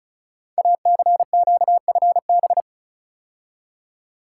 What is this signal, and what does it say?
Morse code, heard as a keyed tone, about 36 wpm, 690 Hz. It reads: ACQFB
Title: Code Groups - 5 Letters - 35wpm